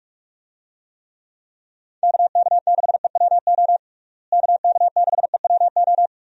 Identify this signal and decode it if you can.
Morse code, heard as a keyed tone, about 46 wpm, 700 Hz. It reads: KK6EWO KK6EWO
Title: USA Call Signs - 6 Characters - Repeated 2x - 45wpm